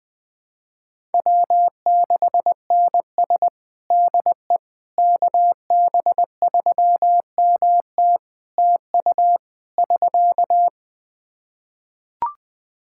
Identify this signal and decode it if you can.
Morse code, heard as a keyed tone, about 20 wpm, 700 Hz. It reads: W6NS DE KB3MT TU <SK>
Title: Practice QSO - Exchange 4 - Ending - 20wpm